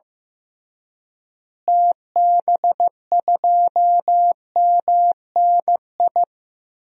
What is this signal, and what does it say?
Morse code, heard as a keyed tone, about 15 wpm, 700 Hz. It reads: TB2MNI
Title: DXCC-100 Call Signs - 6 Characters - Repeated 2x - 15wpm